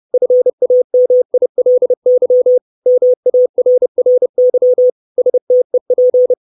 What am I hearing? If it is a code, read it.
Morse code, heard as a keyed tone, about 30 wpm, 500 Hz. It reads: FAMILY MARRY STEP
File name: 632 Words 30WPM